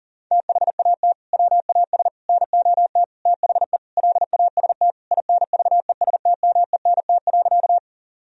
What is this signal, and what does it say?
Morse code, heard as a keyed tone, about 40 wpm, 700 Hz. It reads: THAT WAS NOT THE LAST INVESTMENT.
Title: Sentences from Top 1000 Words - Review Part 4 - 40wpm